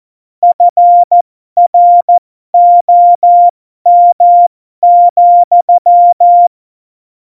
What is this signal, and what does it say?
Morse code, heard as a keyed tone, about 14 wpm, spaced as for 12 wpm, 700 Hz. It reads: FROM,
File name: Morsekurs / Witze